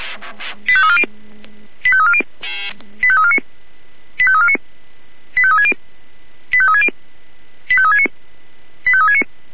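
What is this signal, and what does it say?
Five-tone selective call (ZVEI), heard as quick runs of five tones, tones of about 70 ms. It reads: 85340 85340 85349 85349 85360 85360 85369 85369
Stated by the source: ZVEI five-tone selective call received on SDR.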